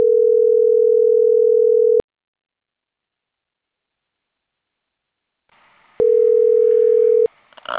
Ringback tone (North America), 440 + 480 Hz.